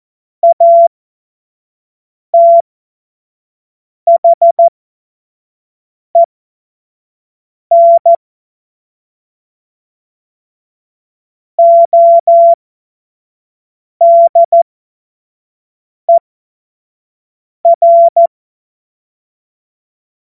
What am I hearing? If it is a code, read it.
Morse code, heard as a keyed tone, about 14 wpm, spaced as for 5 wpm, 680 Hz. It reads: ATHEN ODER
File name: Morsekurs / Witze / Witz 07. Februar 2023 25 F70